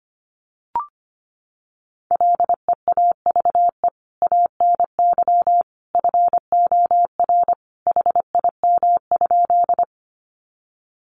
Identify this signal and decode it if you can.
Morse code, heard as a keyed tone, about 25 wpm, 700 Hz. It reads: LEAVE ANY FOR HIM?
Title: Sentences from Top 400 Words 25wpm